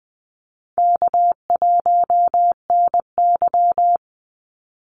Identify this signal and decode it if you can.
Morse code, heard as a keyed tone, about 20 wpm, 700 Hz. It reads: K1NY
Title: Easy Call Signs Repeated 3x - 20wpm